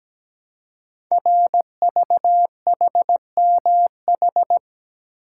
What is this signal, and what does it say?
Morse code, heard as a keyed tone, about 17 wpm, 700 Hz. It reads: RVHMH
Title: Code Groups - 5 Letters - 17wpm